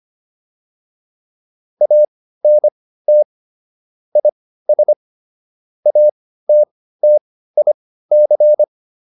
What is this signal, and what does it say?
Morse code, heard as a keyed tone, about 25 wpm, spaced as for 15 wpm, 600 Hz. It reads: ANT IS ATTIC